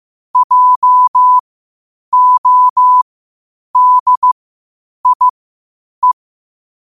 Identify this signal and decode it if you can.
Morse code, heard as a keyed tone, about 15 wpm, spaced as for 8 wpm, 1000 Hz. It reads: JODIE
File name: qotdfiles / 8wpm